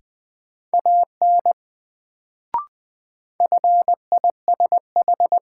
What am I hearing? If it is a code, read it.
Morse code, heard as a keyed tone, about 21 wpm, 700 Hz. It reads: AN FISH
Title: Sentences from Top 300 Words - New Word Drill - 20wpm